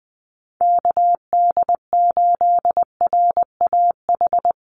Morse code, heard as a keyed tone, about 20 wpm, 700 Hz. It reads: KD8RAH
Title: USA Call Signs 20wpm